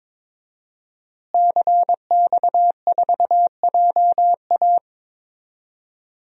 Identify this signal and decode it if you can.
Morse code, heard as a keyed tone, about 22 wpm, 700 Hz. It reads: CX4JA
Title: DXCC-100 Call Signs - 5 Characters - Repeated 2x - 22wpm